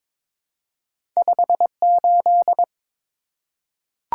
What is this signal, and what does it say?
Morse code, heard as a keyed tone, about 22 wpm, 700 Hz. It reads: 58
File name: practice / 22wpm / Two Letter-Number Combinations - 22wpm